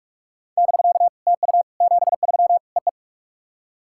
Morse code, heard as a keyed tone, about 44 wpm, 690 Hz. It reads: <BK> TU 73 EE
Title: SOTA QSO Practice 45wpm